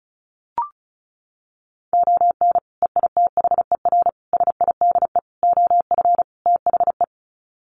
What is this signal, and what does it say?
Morse code, heard as a keyed tone, about 34 wpm, 700 Hz. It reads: ON EITHER SIDE OF THE
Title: Sets of 5 Words 35wpm